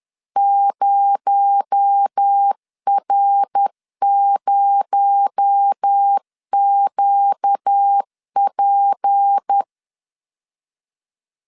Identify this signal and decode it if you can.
Morse code, heard as a keyed tone, about 10 wpm, 800 Hz. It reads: 0R0QP
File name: LEZIONI / 6 PKX0 / 6 PKX0 10wpm 2707